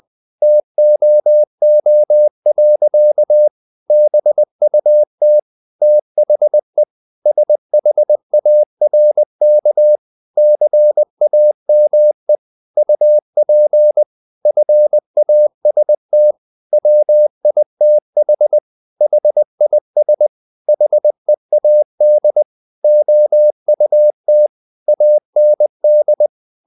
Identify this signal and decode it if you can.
Morse code, heard as a keyed tone, about 20 wpm, 600 Hz. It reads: TOO. BUT THE SHARK CAME UP FAST WITH HIS HEAD OUT AND
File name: the old man and the sea part 2 20wpm